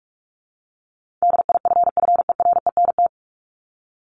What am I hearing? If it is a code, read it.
Morse code, heard as a keyed tone, about 44 wpm, 700 Hz. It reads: DIFFERENT
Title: Top 501-750 Words 45wpm